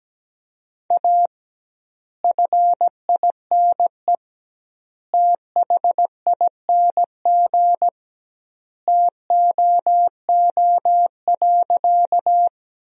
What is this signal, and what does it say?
Morse code, heard as a keyed tone, about 17 wpm, 700 Hz. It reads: A FINE THING TOO.
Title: 2x Word-Spacing - Sentences from Top 500 Words - 17wpm